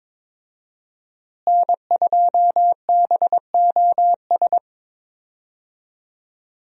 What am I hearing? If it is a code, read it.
Morse code, heard as a keyed tone, about 22 wpm, 700 Hz. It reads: N2BOS
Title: USA Call Signs - 5 Characters - Repeated 3x - 22wpm